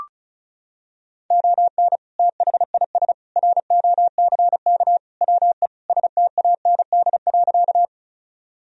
Morse code, heard as a keyed tone, about 34 wpm, 700 Hz. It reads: ON THIS ROCK WE STAND.
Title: Sentences from Top 1000 Words - Review Part 2 - 35wpm